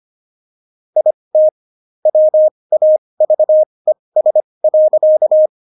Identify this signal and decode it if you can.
Morse code, heard as a keyed tone, about 25 wpm, spaced as for 20 wpm, 620 Hz. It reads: IT WAVES.
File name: Jokes-25-char-20-eff